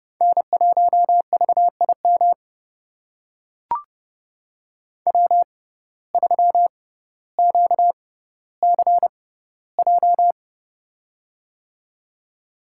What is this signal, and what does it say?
Morse code, heard as a keyed tone, about 30 wpm, spaced as for 15 wpm, 700 Hz. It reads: N1VIM W3QCJ
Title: USA Call Signs - 5 Characters - 6x Character Spacing - 30wpm